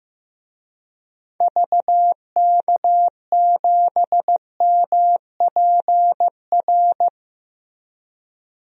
Morse code, heard as a keyed tone, about 15 wpm, 700 Hz. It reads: VK7MPR
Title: DXCC-100 Call Signs - 6 Characters - 15wpm